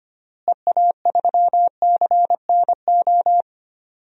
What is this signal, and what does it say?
Morse code, heard as a keyed tone, about 25 wpm, 700 Hz. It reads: EA3CNO
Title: DXCC-100 Call Signs - 6 Characters - Repeated 2x - 25wpm